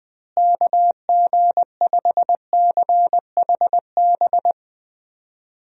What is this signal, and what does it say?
Morse code, heard as a keyed tone, about 20 wpm, 700 Hz. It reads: KG5CHB
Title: USA Call Signs - 6 Characters - Repeated 2x - 20wpm